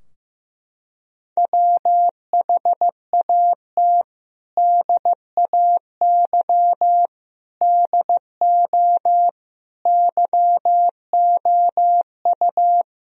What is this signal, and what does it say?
Morse code, heard as a keyed tone, about 14 wpm, 700 Hz. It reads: WHAT DAY DO YOU
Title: Sentences from Top 1000 Words - Review Part 3 - 15wpm